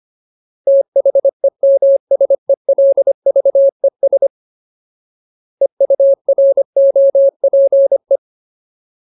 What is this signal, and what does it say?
Morse code, heard as a keyed tone, about 25 wpm, 550 Hz. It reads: THEMSELVES EUROPE
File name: Ultimate Word List / 25WPM Ultimate